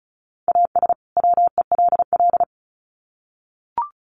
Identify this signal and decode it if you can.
Morse code, heard as a keyed tone, about 34 wpm, 700 Hz. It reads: AS WELL
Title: Sets of 2 Words 35wpm